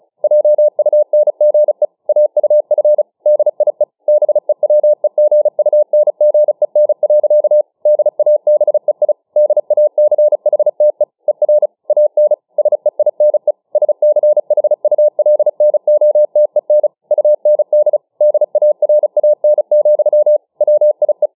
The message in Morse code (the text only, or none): JUNGE AUF DIE BEWEGUNGEN. DABEI DACHTE ER AN SEINE SCHULNOTEN UND DARAN, WIE